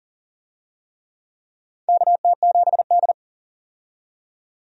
Morse code, heard as a keyed tone, about 40 wpm, 700 Hz. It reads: KT7D
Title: USA Call Signs - 4 Characters - 40wpm